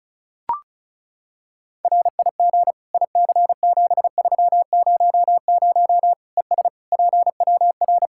Morse code, heard as a keyed tone, about 34 wpm, 700 Hz. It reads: RIG IC7300 ES PWR
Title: Practice QSO - Exchange 2 - Info - 35wpm